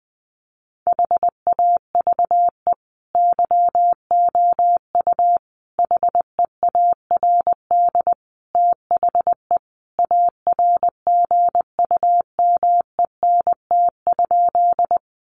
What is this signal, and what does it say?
Morse code, heard as a keyed tone, about 20 wpm, 700 Hz. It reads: HAVE YOU HEARD THE ARGUMENT?
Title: Sentences from Top 1400 Words 20wpm